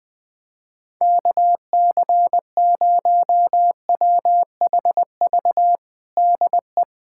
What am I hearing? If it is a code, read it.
Morse code, heard as a keyed tone, about 20 wpm, 700 Hz. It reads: KC0WHV DE
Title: USA Callsign DE Callsign 20wpm